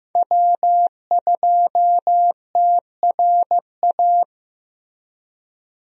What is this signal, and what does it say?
Morse code, heard as a keyed tone, about 15 wpm, 700 Hz. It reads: W2TRA
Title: USA Call Signs - 5 Characters - Repeated 3x - 15wpm